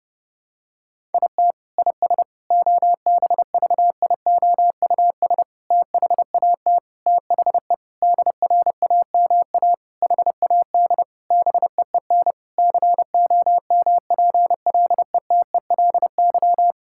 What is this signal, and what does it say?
Morse code, heard as a keyed tone, about 30 wpm, 700 Hz. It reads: IT IS OBVIOUS THAT THE DRAMA HAD BEEN COMPLETELY
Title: Sentences from Top 1600 Words 30wpm